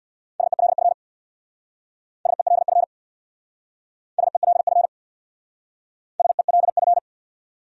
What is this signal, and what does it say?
Morse code, heard as a keyed tone, about 60 wpm, 700 Hz. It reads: SELF SELF SELF SELF
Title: Speed-Racing Top 500 Words 50wpm